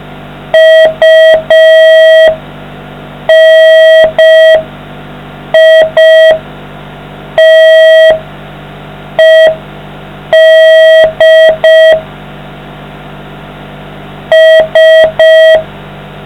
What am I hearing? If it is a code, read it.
Morse code, heard as a keyed tone, about 5 wpm, spaced as for 4 wpm, 640 Hz. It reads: UNITED S